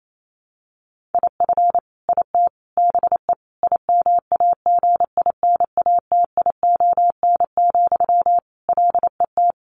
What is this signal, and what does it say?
Morse code, heard as a keyed tone, about 28 wpm, 700 Hz. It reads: IF IT BE IMAGINATION, LET
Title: Sentences from Top 1300 Words 28wpm